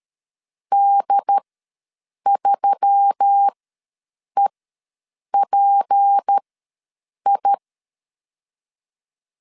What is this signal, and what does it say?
Morse code, heard as a keyed tone, about 13 wpm, spaced as for 7 wpm, 800 Hz. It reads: D3EPI
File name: 7 GF9Z ws 12wpm